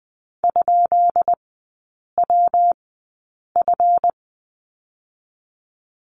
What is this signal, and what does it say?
Morse code, heard as a keyed tone, about 20 wpm, 700 Hz. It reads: ? W F
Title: 212 Lesson 34e - Number 8 - Three Characters 20wpm